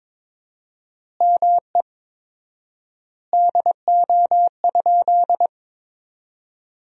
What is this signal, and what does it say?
Morse code, heard as a keyed tone, about 22 wpm, 700 Hz. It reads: ME DO?
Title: 4x Word-Spacing - Sentences from Top 200 Words - 22wpm